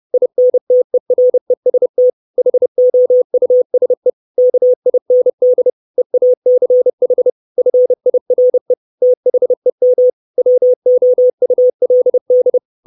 Morse code, heard as a keyed tone, about 30 wpm, 500 Hz. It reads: INTEREST HOUSE KIND EACH FIRE THEM WOULD
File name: 632 Words 30WPM